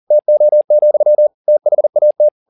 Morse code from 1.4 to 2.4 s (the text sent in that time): THAT